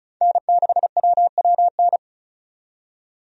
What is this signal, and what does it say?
Morse code, heard as a keyed tone, about 34 wpm, 700 Hz. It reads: N6WWN
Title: USA Call Signs - 5 Characters - 35wpm